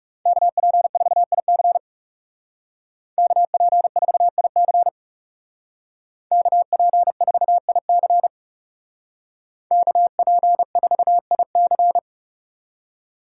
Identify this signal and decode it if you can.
Morse code, heard as a keyed tone, about 37 wpm, 700 Hz. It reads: KP4IC KP4IC KP4IC KP4IC
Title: Speed Racing - USA 2x2 Call Signs 30wpm